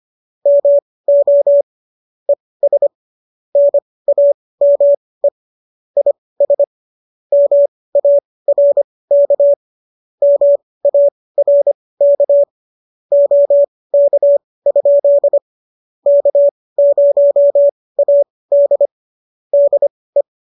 Morse code, heard as a keyed tone, about 25 wpm, spaced as for 18 wpm, 570 Hz. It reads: MO ES NAME IS MARK MARK OK? K0AD DE